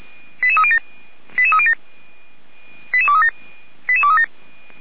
Five-tone selective call (ZVEI), heard as quick runs of five tones, tones of about 70 ms. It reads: 88307 88307 88237 88237